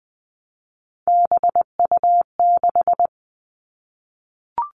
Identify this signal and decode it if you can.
Morse code, heard as a keyed tone, about 19 wpm, 700 Hz. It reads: BU6
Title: Three Challenging Letter-Number Combinations 20wpm